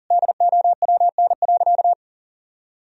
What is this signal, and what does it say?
Morse code, heard as a keyed tone, about 40 wpm, 700 Hz. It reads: DOWN.
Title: Sentences from Top 1000 Words - Review Part 2 - 40wpm